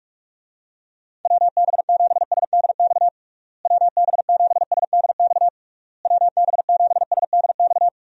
Morse code, heard as a keyed tone, about 47 wpm, 700 Hz. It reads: WB7SDX WB7SDX WB7SDX
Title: USA Call Signs - 6 Characters - Repeated 3x - 45wpm